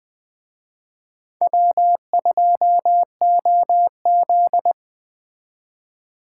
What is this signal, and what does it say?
Morse code, heard as a keyed tone, about 20 wpm, 700 Hz. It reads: W2OZ
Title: Speed Racing - USA 1x2 Call Signs 17wpm